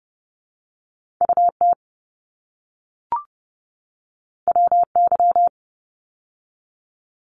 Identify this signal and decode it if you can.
Morse code, heard as a keyed tone, about 31 wpm, 700 Hz. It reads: UT WY
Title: ARRL Sections 30wpm